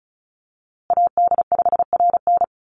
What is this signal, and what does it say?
Morse code, heard as a keyed tone, about 34 wpm, 700 Hz. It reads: AD5RN
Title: USA Call Signs Repeated 2x 35wpm